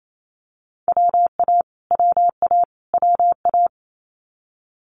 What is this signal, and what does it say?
Morse code, heard as a keyed tone, about 28 wpm, 700 Hz. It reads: WA WA WA
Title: Multi-letter Phonemes Repeated 3x 28wpm